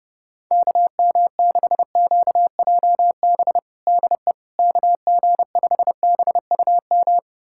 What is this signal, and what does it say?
Morse code, heard as a keyed tone, about 30 wpm, 700 Hz. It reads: KM6QJB DE KG5BUM
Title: USA Callsign DE Callsign - 6 Characters - 30wpm